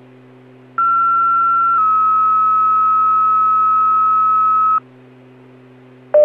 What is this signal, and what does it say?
Two-tone page, heard as a one-second tone, then a three-second tone, about 1360 then 1290 Hz. It is heard from 0.8 to 4.8 s.